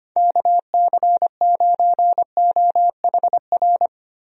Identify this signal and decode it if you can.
Morse code, heard as a keyed tone, about 25 wpm, 700 Hz. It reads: KC9OHR
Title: USA Call Signs - 6 Characters - 25wpm